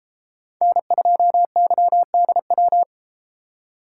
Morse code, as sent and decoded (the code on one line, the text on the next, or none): -. ..--- -.-- -.. .--
N2YDW